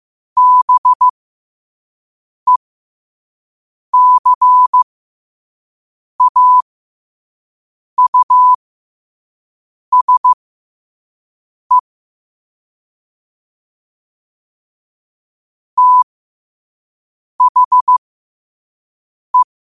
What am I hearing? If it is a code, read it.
Morse code, heard as a keyed tone, about 15 wpm, spaced as for 5 wpm, 1000 Hz. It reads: BECAUSE THE
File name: qotdfiles / 5wpm